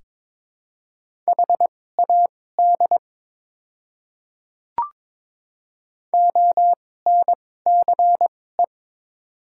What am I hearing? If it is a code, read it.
Morse code, heard as a keyed tone, about 22 wpm, spaced as for 16 wpm, 700 Hz. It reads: HAD ONCE
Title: 2x Character Spacing - Top 300 Words - 22wpm